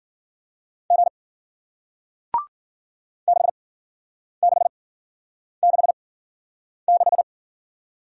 Morse code, heard as a keyed tone, about 56 wpm, spaced as for 9 wpm, 700 Hz. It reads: G 6666